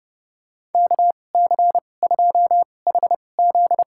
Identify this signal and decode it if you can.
Morse code, heard as a keyed tone, about 30 wpm, spaced as for 22 wpm, 700 Hz. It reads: KC2HZ
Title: USA Call Signs - 2x Character Spacing - 30wpm